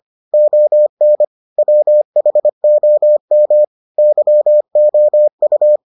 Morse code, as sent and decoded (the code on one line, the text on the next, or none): --- -. / .-- .... --- -- / -.-- --- ..-
ON WHOM YOU